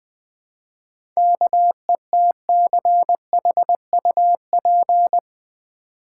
Morse code, heard as a keyed tone, about 20 wpm, 700 Hz. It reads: KETCHUP
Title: Easy Words with Hints - 20wpm